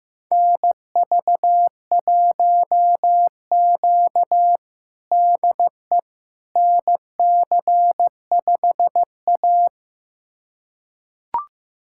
Morse code, heard as a keyed tone, about 15 wpm, 700 Hz. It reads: NV1Q DE NC5A
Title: USA Callsign DE Callsign - 4 Characters - 15wpm